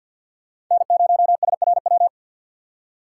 Morse code, heard as a keyed tone, about 52 wpm, 700 Hz. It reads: N0SRW
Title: USA Call Signs - 5 Characters - 2x Character Spacing - 50wpm